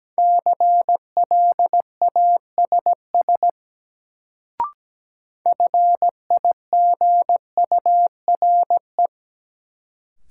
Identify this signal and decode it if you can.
Morse code, heard as a keyed tone, about 17 wpm, 700 Hz. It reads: CLASS FIGURE
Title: Sentences from Top 1000 Words - Review Part 2 - New Word Drill - 17wpm